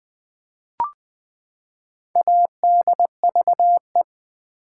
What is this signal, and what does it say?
Morse code, heard as a keyed tone, about 20 wpm, 700 Hz. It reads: ADVE
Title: Build A Word - 4 Letters At A Time - 11-Letter Words - 20wpm